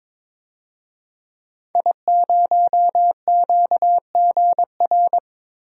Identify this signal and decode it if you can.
Morse code, heard as a keyed tone, about 22 wpm, 700 Hz. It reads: I0QGR
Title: DXCC-100 Call Signs - 5 Characters - Repeated 2x - 22wpm